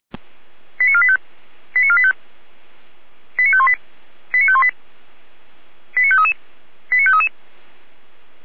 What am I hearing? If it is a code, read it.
Five-tone selective call (ZVEI), heard as quick runs of five tones, tones of about 70 ms. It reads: 89485 89485 89518 89518 89533 89533